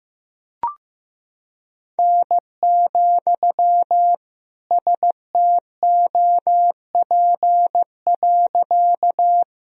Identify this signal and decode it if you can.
Morse code, heard as a keyed tone, about 15 wpm, 700 Hz. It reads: N, STOP.
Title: Sentences from Top 300 Words and Abbreviations 15wpm